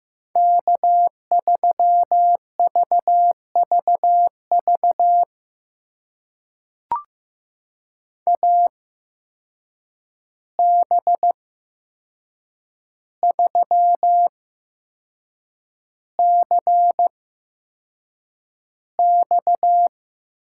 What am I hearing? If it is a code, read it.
Morse code, heard as a keyed tone, about 15 wpm, spaced as for 6 wpm, 700 Hz. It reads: K3VVV AB3CX
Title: USA Call Signs - 5 Characters - 8x Character Spacing - 15wpm